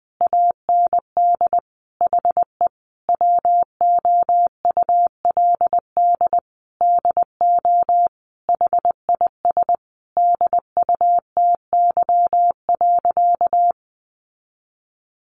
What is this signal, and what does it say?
Morse code, heard as a keyed tone, about 19 wpm, 700 Hz. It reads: AND HE WOULD DO HIS DUTY.
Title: Sentences from Top 1500 Words 20wpm